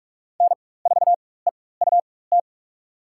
Morse code, heard as a keyed tone, about 47 wpm, spaced as for 21 wpm, 700 Hz. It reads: N4EUT
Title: USA Call Signs - 5 Characters - 4x Character Spacing - 45wpm